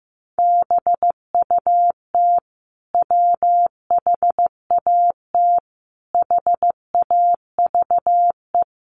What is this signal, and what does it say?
Morse code, heard as a keyed tone, about 15 wpm, 700 Hz. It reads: BUT WHAT HAVE
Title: Sentences from Top 100 Words 15wpm